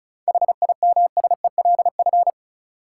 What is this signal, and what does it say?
Morse code, heard as a keyed tone, about 36 wpm, 700 Hz. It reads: HIMSELF
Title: Sentences from Top 1500 Words - New Word Drill Alt - 35wpm